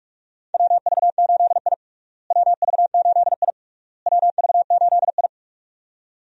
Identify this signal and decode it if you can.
Morse code, heard as a keyed tone, about 43 wpm, 700 Hz. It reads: WV8I WV8I WV8I
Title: USA Call Signs - 4 Characters - Repeated 3x - 45wpm